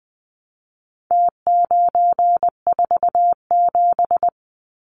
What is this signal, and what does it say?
Morse code, heard as a keyed tone, about 20 wpm, 700 Hz. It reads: T947
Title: Numbers - Contest Style - 4-Digit 20wpm